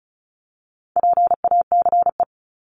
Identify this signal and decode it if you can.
Morse code, heard as a keyed tone, about 34 wpm, 700 Hz. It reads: PACE
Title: Top 1751-2000 Words 35wpm